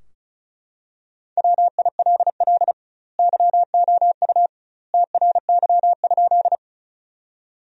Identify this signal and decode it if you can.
Morse code, heard as a keyed tone, about 34 wpm, 700 Hz. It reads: WILL YOU TRY?
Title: 2x Word-Spacing - Sentences from Top 200 Words - 35wpm